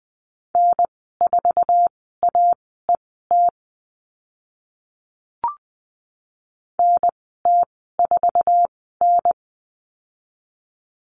Morse code, heard as a keyed tone, about 20 wpm, spaced as for 14 wpm, 700 Hz. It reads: N4AET NT4N
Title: 055 Lesson 10g - Letter r - Call Signs 20wpm